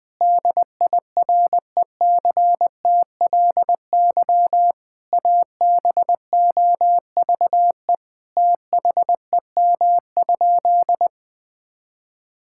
Morse code, heard as a keyed tone, about 21 wpm, 700 Hz. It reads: DIRECTLY ABOVE THEM?
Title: Sentences from Top 1000 Words - Review Part 5 - 20wpm